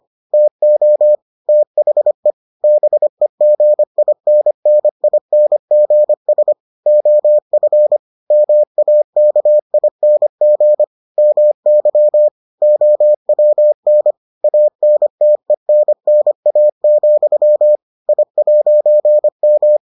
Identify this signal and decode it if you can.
Morse code, heard as a keyed tone, about 25 wpm, 600 Hz. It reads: TO THE BEGINNINGS OF MAKING MY OWN ANTENNA, I'M